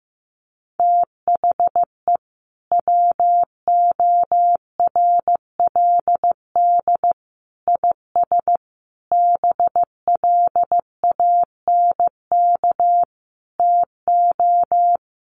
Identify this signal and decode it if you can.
Morse code, heard as a keyed tone, about 15 wpm, 700 Hz. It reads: THE WORLD IS BLANK TO
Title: Sentences from Top 1500 Words 15wpm